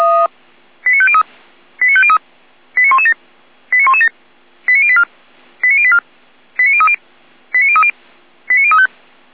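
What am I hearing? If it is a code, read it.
Five-tone selective call (ZVEI), heard as quick runs of five tones, tones of about 70 ms. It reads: 89503 89503 89117 89117 89084 89084 89030 89030 89035
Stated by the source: ZVEI five-tone selective call received on an SDR.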